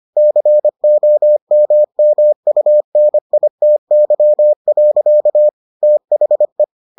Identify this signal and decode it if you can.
Morse code, heard as a keyed tone, about 25 wpm, 600 Hz. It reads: COMMUNITY. THE